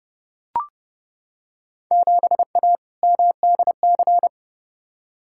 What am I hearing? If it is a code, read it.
Morse code, heard as a keyed tone, about 30 wpm, 700 Hz. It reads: 7A MDC
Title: Field Day Exchanges Encore - 30wpm